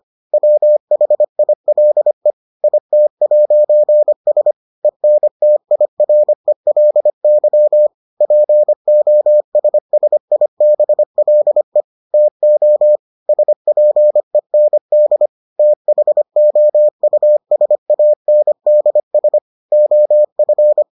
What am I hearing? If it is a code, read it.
Morse code, heard as a keyed tone, about 25 wpm, 600 Hz. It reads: WHILE IT'S ENTIRELY POSSIBLE TO SPEND THOUSANDS OF